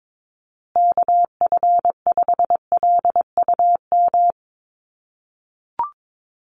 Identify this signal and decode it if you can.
Morse code, heard as a keyed tone, about 22 wpm, 700 Hz. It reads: KF5LUM
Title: USA Call Signs Repeated 2x 22wpm